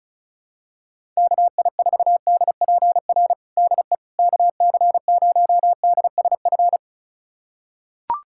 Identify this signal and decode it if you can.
Morse code, heard as a keyed tone, about 36 wpm, 700 Hz. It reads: KI4DPR DE KC0DSF
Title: USA Callsign DE Callsign 35wpm